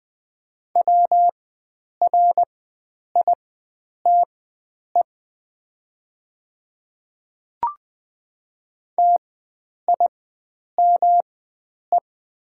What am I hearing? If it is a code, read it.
Morse code, heard as a keyed tone, about 21 wpm, spaced as for 9 wpm, 700 Hz. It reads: WRITE TIME
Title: 4x Character Spacing - Top 100 Words - 20wpm